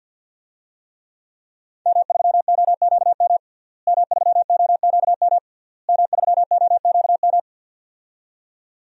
Morse code, heard as a keyed tone, about 50 wpm, 700 Hz. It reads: M3OQM M3OQM M3OQM
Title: DXCC-100 Call Signs - 5 Characters - Repeated 3x - 50wpm